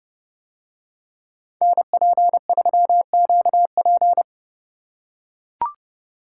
Morse code, heard as a keyed tone, about 30 wpm, 700 Hz. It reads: NP3QP